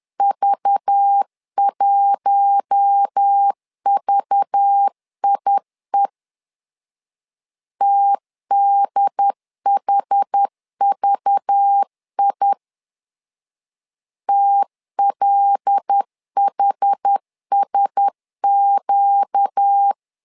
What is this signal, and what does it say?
Morse code, heard as a keyed tone, about 10 wpm, 800 Hz. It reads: V1VIE TDHVI TLHSQ
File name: LEZIONI / 5 LBVC / 5 LBV 10wpm 2248